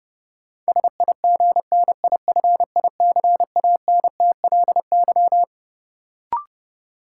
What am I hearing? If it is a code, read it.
Morse code, heard as a keyed tone, about 32 wpm, 700 Hz. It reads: SIGNIFICANTLY
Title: Sentences from Top 1500 Words - New Word Drill - 30wpm